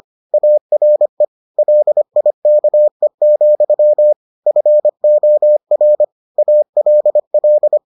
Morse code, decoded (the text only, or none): ARE LIKE, FOR ALL